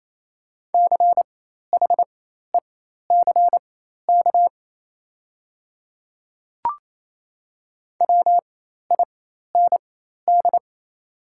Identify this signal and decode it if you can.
Morse code, heard as a keyed tone, about 28 wpm, spaced as for 13 wpm, 690 Hz. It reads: CHECK WIND
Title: 4x Character Spacing - Top 500 Words - 28wpm